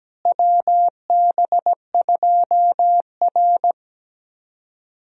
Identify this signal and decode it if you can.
Morse code, heard as a keyed tone, about 17 wpm, 700 Hz. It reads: WB2R
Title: USA Call Signs - 4 Characters - Repeated 3x - 17wpm